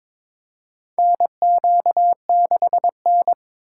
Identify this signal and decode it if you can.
Morse code, heard as a keyed tone, about 22 wpm, 700 Hz. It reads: NQ6N
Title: USA Call Signs - 4x Character Spacing - 22wpm